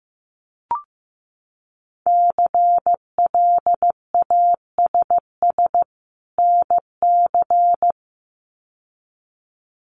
Morse code, heard as a keyed tone, about 15 wpm, 700 Hz. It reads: CLASS NC
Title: Winter Field Day Exchanges 15wpm